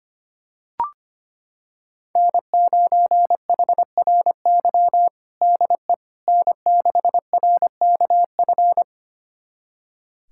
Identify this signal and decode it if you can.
Morse code, heard as a keyed tone, about 25 wpm, 700 Hz. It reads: N9HRY DE N6RKF
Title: USA Callsign DE Callsign - 5 Characters - 25wpm